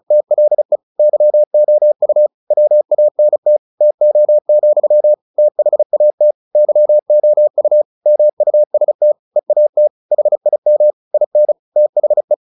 Morse code, heard as a keyed tone, about 34 wpm, 600 Hz. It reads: TLE YOU WANT TO, THAT YOU MUST EAT HIM IN THE
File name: the old man and the sea / the old man and the sea 1 35wpm